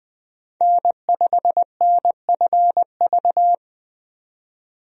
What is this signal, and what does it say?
Morse code, heard as a keyed tone, about 20 wpm, 700 Hz. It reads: N5NFV
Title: USA Call Signs - 5 Characters - 20wpm